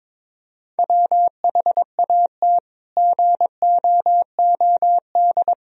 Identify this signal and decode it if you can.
Morse code, heard as a keyed tone, about 22 wpm, 700 Hz. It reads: WHAT GOOD
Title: Sentences from Top 1000 Words - Review Part 3 - 22wpm